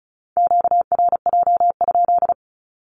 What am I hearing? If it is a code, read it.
Morse code, heard as a keyed tone, about 36 wpm, 700 Hz. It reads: QRJ?
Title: Q-Codes with Spoken Meaning - 35wpm